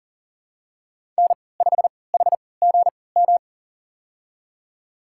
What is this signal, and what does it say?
Morse code, heard as a keyed tone, about 40 wpm, spaced as for 23 wpm, 700 Hz. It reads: N5HGM
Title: USA Call Signs - 5 Characters - 3x Character Spacing - 40wpm